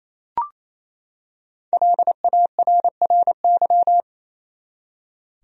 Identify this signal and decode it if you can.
Morse code, heard as a keyed tone, about 28 wpm, 700 Hz. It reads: LARRY
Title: USA Common Names - Max 5 Letters - 28wpm